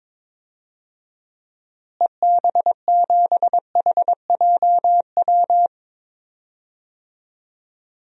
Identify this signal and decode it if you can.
Morse code, heard as a keyed tone, about 21 wpm, 700 Hz. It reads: EB7HJW
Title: DXCC-100 Call Signs - 6 Characters - Repeated 2x - 22wpm